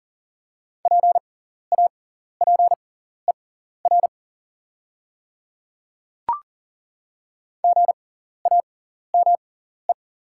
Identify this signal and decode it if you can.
Morse code, heard as a keyed tone, about 39 wpm, spaced as for 14 wpm, 690 Hz. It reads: PAPER GAME
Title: 6x Character Spacing - Top 500 Words - 40wpm